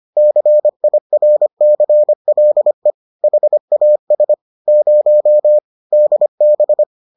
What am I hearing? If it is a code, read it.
Morse code, heard as a keyed tone, about 25 wpm, 600 Hz. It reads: CIRCLE HAS 0 DB